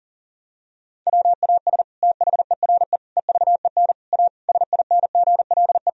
Morse code, heard as a keyed tone, about 40 wpm, 700 Hz. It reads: WAS THERE EVEN A SINGLE
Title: Sentences from Top 1000 Words - Review Part 4 - 40wpm